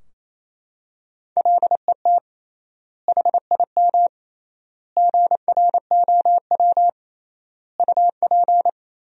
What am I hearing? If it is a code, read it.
Morse code, heard as a keyed tone, about 28 wpm, 700 Hz. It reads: LET HIM GROW UP
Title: 3x Word-Spacing - Sentences from Top 300 Words - 28wpm